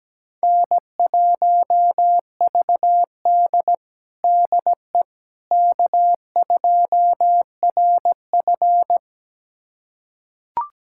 Morse code, heard as a keyed tone, about 17 wpm, 700 Hz. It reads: N1VD DE K2RF
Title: USA Callsign DE Callsign - 4 Characters - 17wpm